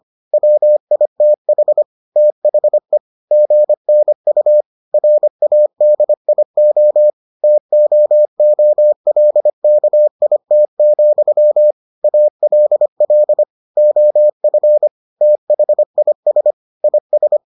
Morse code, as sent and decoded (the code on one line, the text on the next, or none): .-- .. - .... / - .... / --. -. ..- / .-. .- -.. .. --- / - --- --- .-.. -.- .. - --..-- / .- .-.. .-.. / --- ..-. / - .... .. ... / .. ...
WITH THE GNU RADIO TOOLKIT, ALL OF THIS IS